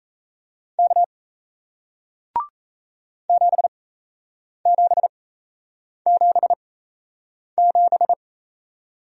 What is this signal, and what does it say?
Morse code, heard as a keyed tone, about 38 wpm, spaced as for 8 wpm, 700 Hz. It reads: K 7777